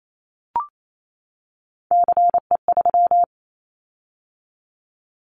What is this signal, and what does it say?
Morse code, heard as a keyed tone, about 28 wpm, 700 Hz. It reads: CE3
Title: Callsign Prefixes for Top 100 DXCC - 28wpm